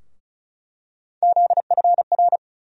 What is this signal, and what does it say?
Morse code, heard as a keyed tone, about 34 wpm, 700 Hz. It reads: ZFR
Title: USA Call Sign Postfixes - 3 Letters 35wpm